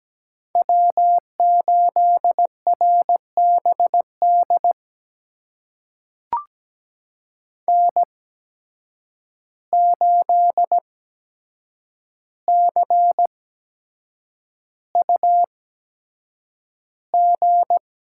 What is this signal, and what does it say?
Morse code, heard as a keyed tone, about 17 wpm, spaced as for 7 wpm, 700 Hz. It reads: W8RBD N8CUG
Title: USA Call Signs - 5 Characters - 8x Character Spacing - 17wpm